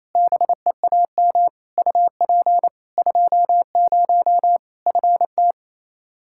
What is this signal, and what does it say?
Morse code, heard as a keyed tone, about 28 wpm, 700 Hz. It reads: BEAM UP 20 FT
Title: Practice QSO - Exchange 2 - Info - 28wpm